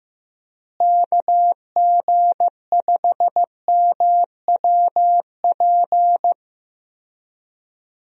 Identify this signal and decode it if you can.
Morse code, heard as a keyed tone, about 15 wpm, 700 Hz. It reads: KG5MWP
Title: USA Call Signs - 6 Characters - Repeated 3x - 15wpm